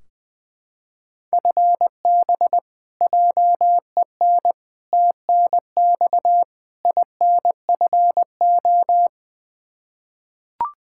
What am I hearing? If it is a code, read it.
Morse code, heard as a keyed tone, about 20 wpm, 700 Hz. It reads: FB JEN TNX INFO
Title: Practice QSO - Exchange 3 - More Info - 20wpm